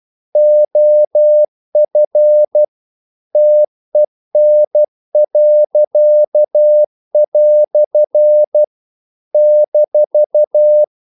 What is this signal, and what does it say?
Morse code, heard as a keyed tone, about 12 wpm, 600 Hz. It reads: OF TEN." -
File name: CW / Practice / EnglishQuotations0312